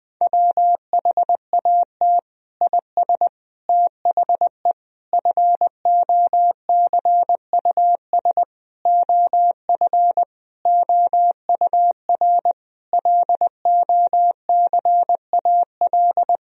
Morse code, heard as a keyed tone, about 20 wpm, 700 Hz. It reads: WHAT IS THE FOCUS OF OUR LOCAL
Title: Sentences from Top 1000 Words - Review Part 4 - 20wpm